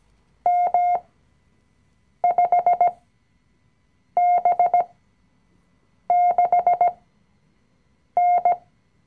Morse code, heard as a keyed tone, about 17 wpm, spaced as for 6 wpm, 710 Hz. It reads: M5B6N
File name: CW-Online-Kurs / Anfänger / Lektion 020